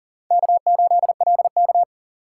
Morse code, heard as a keyed tone, about 39 wpm, 700 Hz. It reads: K8LK
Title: USA Call Signs - 4 Characters - 3x Character Spacing - 40wpm